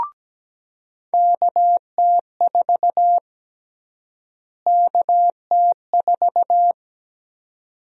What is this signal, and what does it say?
Morse code, heard as a keyed tone, about 17 wpm, 700 Hz. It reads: KT4 KT4
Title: USA Call Sign Prefixes - Repeated 2x - 17wpm